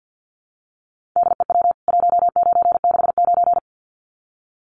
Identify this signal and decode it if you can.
Morse code, heard as a keyed tone, about 49 wpm, 700 Hz. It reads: DEW 1969
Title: CWT Exchange 50wpm